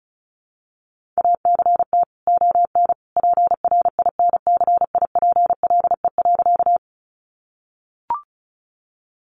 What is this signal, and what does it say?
Morse code, heard as a keyed tone, about 34 wpm, 700 Hz. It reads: ACT ON PRINCIPLE.
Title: Sentences from Top 1400 Words 35wpm